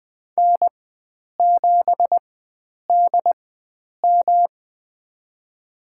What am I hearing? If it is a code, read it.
Morse code, heard as a keyed tone, about 20 wpm, spaced as for 9 wpm, 700 Hz. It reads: N7DM
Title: USA Call Signs - 4 Characters - 4x Character Spacing - 20wpm